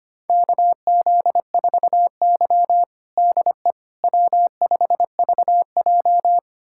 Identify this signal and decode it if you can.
Morse code, heard as a keyed tone, about 25 wpm, 700 Hz. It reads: KZ4Y DE W5VJ
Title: USA Callsign DE Callsign - 4 Characters - 25wpm